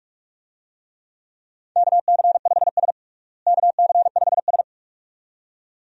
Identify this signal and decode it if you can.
Morse code, heard as a keyed tone, about 46 wpm, 700 Hz. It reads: KC5S KC5S
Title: USA Call Signs - 4 Characters - Repeated 2x - 45wpm